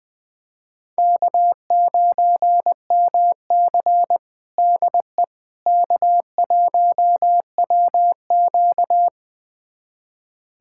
Morse code, heard as a keyed tone, about 20 wpm, 700 Hz. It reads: K9MC DE K1WQ
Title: USA Callsign DE Callsign - 4 Characters - 20wpm